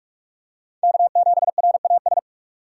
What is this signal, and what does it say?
Morse code, heard as a keyed tone, about 46 wpm, 700 Hz. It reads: K7RAS
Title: USA Call Signs - 5 Characters - 3x Character Spacing - 45wpm